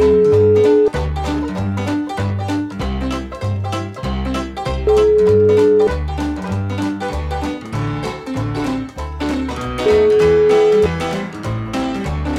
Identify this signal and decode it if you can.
Ringback tone (Europe, 425 Hz), with noise.